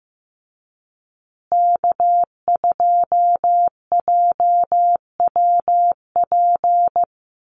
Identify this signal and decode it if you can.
Morse code, heard as a keyed tone, about 15 wpm, 700 Hz. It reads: K2JWP
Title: USA Call Signs Repeated 3x 15wpm